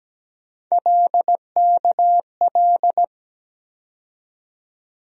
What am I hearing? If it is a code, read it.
Morse code, heard as a keyed tone, about 17 wpm, 700 Hz. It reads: LKL
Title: USA Call Sign Postfixes - 3 Letters 17wpm